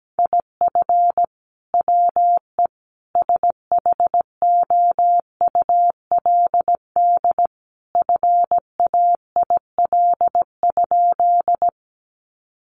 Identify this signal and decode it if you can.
Morse code, heard as a keyed tone, about 17 wpm, 700 Hz. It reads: IF WE SHOULD FAIL?
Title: Sentences from Top 1100 Words 17wpm